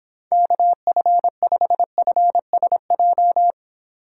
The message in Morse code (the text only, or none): KF5FSJ